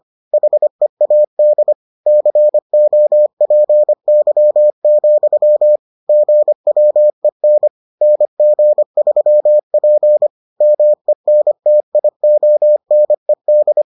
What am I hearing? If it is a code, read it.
Morse code, heard as a keyed tone, about 25 wpm, 600 Hz. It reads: HEAD COPY, GWEN NG3P MENTIONED